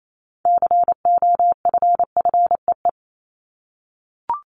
Morse code, heard as a keyed tone, about 28 wpm, 700 Hz. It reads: COFFEE
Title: Top 1751-2000 Words 28wpm